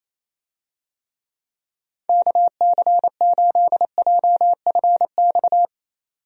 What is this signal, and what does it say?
Morse code, heard as a keyed tone, about 27 wpm, 700 Hz. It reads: KC8JFX
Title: USA Call Signs - 6 Characters - 28wpm